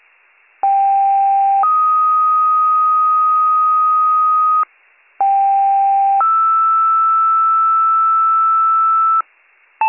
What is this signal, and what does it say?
Two-tone page, heard as a one-second tone, then a three-second tone, about 790 then 1250 Hz, about 790 then 1320 Hz. It is heard from 0.6 to 9.2 s.